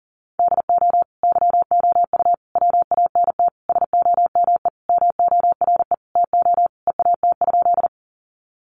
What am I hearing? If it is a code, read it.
Morse code, heard as a keyed tone, about 40 wpm, 700 Hz. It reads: DO YOU WANT SOME MORE TO EAT?
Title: Sentences from Top 300 Words 40wpm